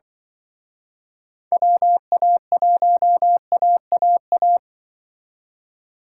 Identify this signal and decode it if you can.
Morse code, heard as a keyed tone, about 25 wpm, 700 Hz. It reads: WA1AAA
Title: Speed Racing - USA 2x3 Call Signs 20wpm